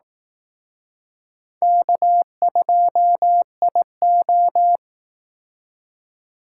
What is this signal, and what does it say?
Morse code, heard as a keyed tone, about 19 wpm, 700 Hz. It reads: K2IO
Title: Speed Racing - USA 1x2 Call Signs 15wpm